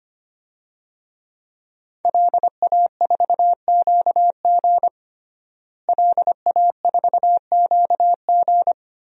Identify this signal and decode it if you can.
Morse code, heard as a keyed tone, about 25 wpm, 700 Hz. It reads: LA4QG LA4QG
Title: DXCC-100 Call Signs - 5 Characters - Repeated 2x - 25wpm